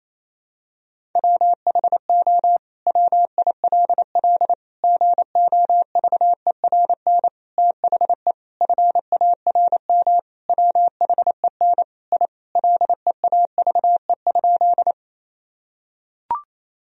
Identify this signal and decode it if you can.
Morse code, heard as a keyed tone, about 28 wpm, 700 Hz. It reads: WHO WILL GOVERN THE FARM WHEN I LEAVE?
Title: Sentences from Top 1000 Words - Review Part 2 - 28wpm